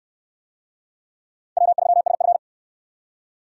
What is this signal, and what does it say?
Morse code, heard as a keyed tone, about 65 wpm, 700 Hz. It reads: W3SF